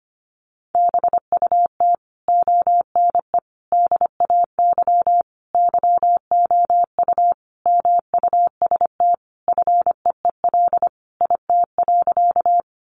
Morse code, heard as a keyed tone, about 25 wpm, 700 Hz. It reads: BUT ONE DAY YOU MUST FEEL IT.
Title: Sentences from Top 400 Words 25wpm